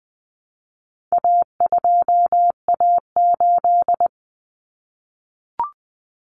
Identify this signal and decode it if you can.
Morse code, heard as a keyed tone, about 20 wpm, 700 Hz. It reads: A2A8
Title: Common Cut Numbers - 4-Digit 20wpm